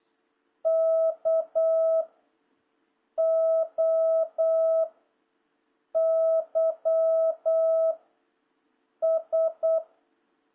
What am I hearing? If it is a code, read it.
Morse code, heard as a keyed tone, about 8 wpm, 650 Hz. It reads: K O Y S
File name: jh8cbh / 2026cw